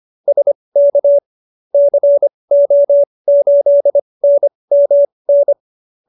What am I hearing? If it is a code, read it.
Morse code, heard as a keyed tone, about 25 wpm, spaced as for 20 wpm, 570 Hz. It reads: SK CO8NMN